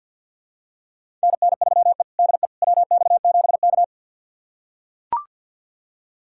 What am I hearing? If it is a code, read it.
Morse code, heard as a keyed tone, about 45 wpm, 700 Hz. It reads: NN3E DE WX7K
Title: USA Callsign DE Callsign - 4 Characters - 50wpm